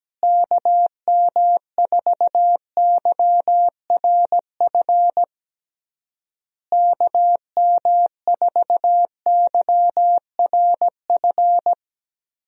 Morse code, heard as a keyed tone, about 17 wpm, 700 Hz. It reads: KM4YRF KM4YRF
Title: USA Call Signs - 6 Characters - Repeated 3x - 17wpm